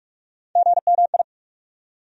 Morse code, heard as a keyed tone, about 42 wpm, 700 Hz. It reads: GMI
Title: USA Call Sign Postfixes - 3 Letters 45wpm